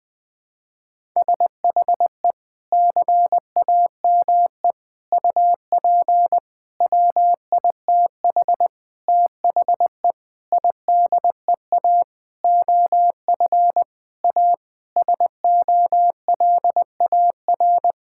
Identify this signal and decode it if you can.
Morse code, heard as a keyed tone, about 20 wpm, 700 Hz. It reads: SHE CAME UP WITH THE IDEA OF A SOLAR
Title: Sentences with 6-Word Phrases - 20wpm